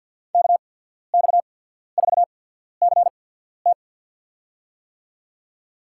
Morse code, heard as a keyed tone, about 53 wpm, spaced as for 14 wpm, 700 Hz. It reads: KX4CT
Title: USA Call Signs - 5 Characters - 8x Character Spacing - 50wpm